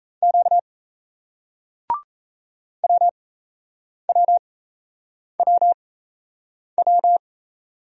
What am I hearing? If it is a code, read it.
Morse code, heard as a keyed tone, about 39 wpm, spaced as for 8 wpm, 700 Hz. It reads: Q WWWW